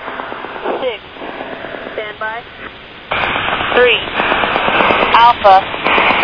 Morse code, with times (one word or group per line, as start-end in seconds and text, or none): none